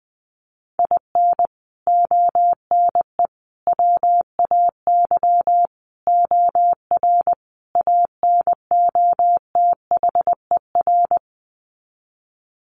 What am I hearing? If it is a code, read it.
Morse code, heard as a keyed tone, about 20 wpm, 700 Hz. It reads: IN ONE WAY OR ANOTHER
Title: Sets of 5 Words Encore 20wpm